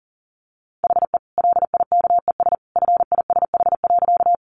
Morse code, heard as a keyed tone, about 40 wpm, 700 Hz. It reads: HE LIKES FISH.
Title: Additional Easy Phrases 40wpm